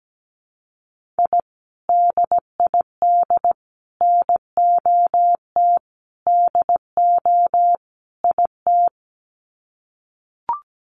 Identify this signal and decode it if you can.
Morse code, heard as a keyed tone, about 17 wpm, 700 Hz. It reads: I DID NOT DO IT
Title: Sets of 5 Words Encore2 17wpm